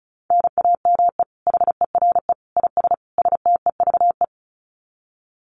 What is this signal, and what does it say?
Morse code, heard as a keyed tone, about 34 wpm, 700 Hz. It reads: NAME HERE IS STEVE
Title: Common QSO Phrases 35wpm